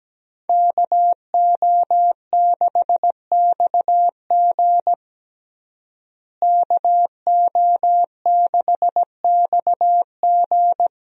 Morse code, heard as a keyed tone, about 17 wpm, 700 Hz. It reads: KO6XG KO6XG
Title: USA Call Signs - 5 Characters - Repeated 3x - 17wpm